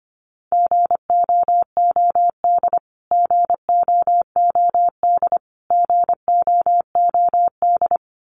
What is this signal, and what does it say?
Morse code, heard as a keyed tone, about 25 wpm, 700 Hz. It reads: GOOD GOOD GOOD
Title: Top 500 Words Repeated 25wpm